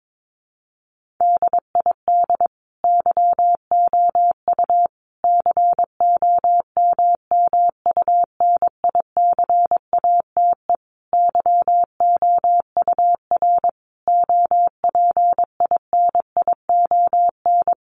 Morse code, heard as a keyed tone, about 22 wpm, 700 Hz. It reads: DID YOU COMMUNICATE YOUR OPINION
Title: Sentences from Top 1300 Words 22wpm